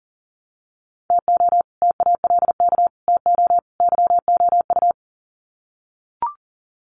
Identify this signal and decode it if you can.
Morse code, heard as a keyed tone, about 38 wpm, 700 Hz. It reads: TO TALK TO YOU
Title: Sets of 4 Words 40wpm